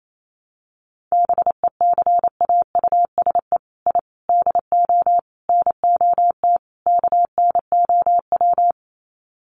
Morse code, heard as a keyed tone, about 28 wpm, 700 Hz. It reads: BECAUSE I DO NOT KNOW
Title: Sets of 5 Words Encore2 28wpm